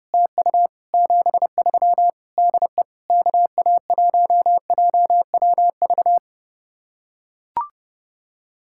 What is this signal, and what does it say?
Morse code, heard as a keyed tone, about 30 wpm, 700 Hz. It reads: TU 73 DE KA1JWV